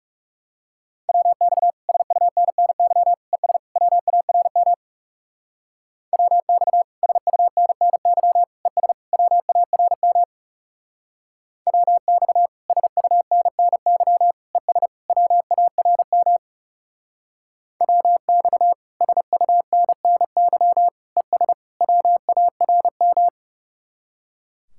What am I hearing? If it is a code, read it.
Morse code, heard as a keyed tone, about 35 wpm, 700 Hz. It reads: WX SUNNY ES WARM WX SUNNY ES WARM WX SUNNY ES WARM WX SUNNY ES WARM
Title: Speed-Racing Common QSO Phrases - 30wpm